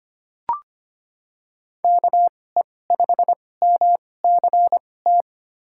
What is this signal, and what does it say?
Morse code, heard as a keyed tone, about 25 wpm, spaced as for 18 wpm, 700 Hz. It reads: KE5MCT
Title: USA Call Signs - 6 Characters - 2x Character Spacing - 25wpm